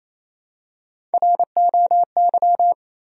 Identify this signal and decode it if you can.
Morse code, heard as a keyed tone, about 28 wpm, 700 Hz. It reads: ROY
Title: USA Common Names - Max 3 Letters - 28wpm